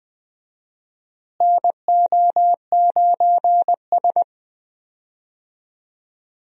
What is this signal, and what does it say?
Morse code, heard as a keyed tone, about 19 wpm, 700 Hz. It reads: NO9S
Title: USA Call Signs - 4 Characters - Repeated 3x - 20wpm